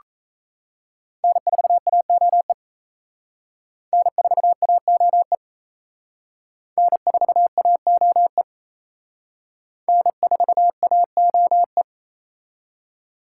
Morse code, heard as a keyed tone, about 36 wpm, 700 Hz. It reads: N4AOE N4AOE N4AOE N4AOE